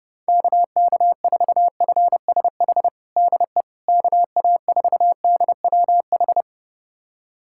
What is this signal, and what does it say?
Morse code, heard as a keyed tone, about 32 wpm, 700 Hz. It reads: KK4FSH DE KA4DWH
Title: USA Callsign DE Callsign - 6 Characters - 30wpm